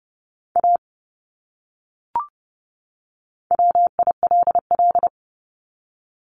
Morse code, heard as a keyed tone, about 30 wpm, 700 Hz. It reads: A WILL
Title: Top 200 Words 30wpm